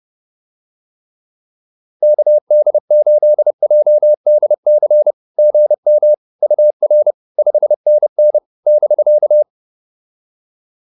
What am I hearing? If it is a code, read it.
Morse code, heard as a keyed tone, about 30 wpm, 600 Hz. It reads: KD8JDC GM UR 5NN <BK>
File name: POTA110 30